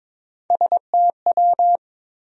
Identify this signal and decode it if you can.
Morse code, heard as a keyed tone, about 22 wpm, 700 Hz. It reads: STW
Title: USA Call Sign Postfixes - 3 Letters 22wpm